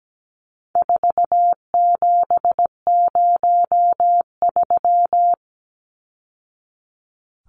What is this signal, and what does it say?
Morse code, heard as a keyed tone, about 17 wpm, 700 Hz. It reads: 4703
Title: Numbers - 4-Digit 17wpm